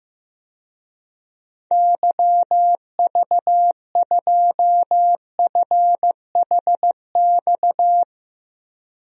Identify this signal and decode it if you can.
Morse code, heard as a keyed tone, about 15 wpm, 700 Hz. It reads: YV2FHX
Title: DXCC-100 Call Signs - 6 Characters - Repeated 2x - 15wpm